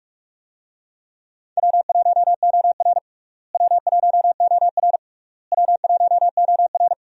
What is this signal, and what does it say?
Morse code, heard as a keyed tone, about 45 wpm, 700 Hz. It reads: W1OR W1OR W1OR
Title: USA Call Signs - 4 Characters - Repeated 3x - 45wpm